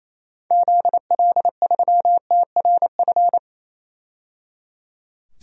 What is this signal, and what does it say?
Morse code, heard as a keyed tone, about 28 wpm, 700 Hz. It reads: ZL3TRF
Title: DXCC-100 Call Signs - 6 Characters - 28wpm